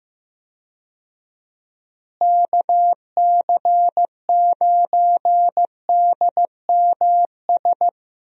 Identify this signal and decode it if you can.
Morse code, heard as a keyed tone, about 15 wpm, 700 Hz. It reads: KC9DMS
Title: USA Call Signs - 6 Characters - Repeated 3x - 15wpm